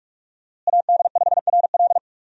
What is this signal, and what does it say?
Morse code, heard as a keyed tone, about 41 wpm, 700 Hz. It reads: AD5RL
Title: USA Call Signs - 5 Characters - Repeated 3x - 45wpm